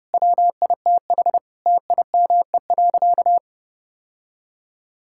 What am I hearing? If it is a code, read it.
Morse code, heard as a keyed tone, about 30 wpm, 700 Hz. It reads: WITH TIME.
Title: Sentences from Top 1000 Words - Review Part 5 - 30wpm